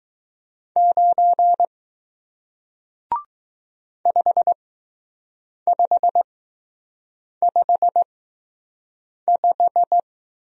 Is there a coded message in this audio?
Yes: Morse code from 0.8 to 10.0 s, heard as a keyed tone, about 19 wpm, spaced as for 6 wpm, 700 Hz. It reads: 9 5555